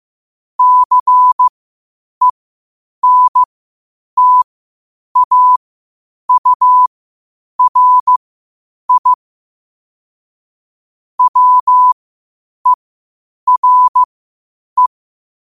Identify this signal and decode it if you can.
Morse code, heard as a keyed tone, about 15 wpm, spaced as for 8 wpm, 1000 Hz. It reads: CENTAURI WERE
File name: qotdfiles / 8wpm